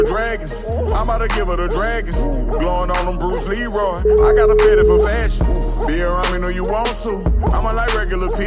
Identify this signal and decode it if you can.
Ringback tone (Europe, 425 Hz), with noise.